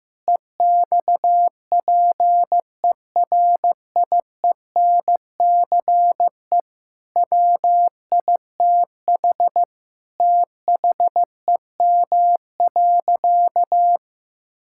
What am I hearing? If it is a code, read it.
Morse code, heard as a keyed tone, about 14 wpm, 700 Hz. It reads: EXPERIENCE WITH THEM.
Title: Sentences from Top 1000 Words - Review Part 6 - 15wpm